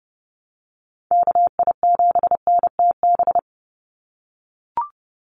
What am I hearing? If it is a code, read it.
Morse code, heard as a keyed tone, about 30 wpm, 700 Hz. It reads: KI7NTB
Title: USA Call Signs 30wpm